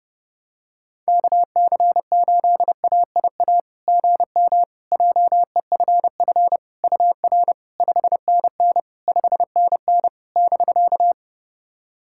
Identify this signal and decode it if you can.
Morse code, heard as a keyed tone, about 30 wpm, 700 Hz. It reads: KC8AIA GM JEFF UR 5NN 5NN <BK>
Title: SOTA QSO Practice 30wpm